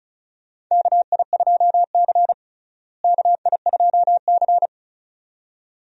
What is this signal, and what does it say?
Morse code, heard as a keyed tone, about 36 wpm, 700 Hz. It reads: KI2C KI2C
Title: USA Call Signs - 4 Characters - Repeated 2x - 35wpm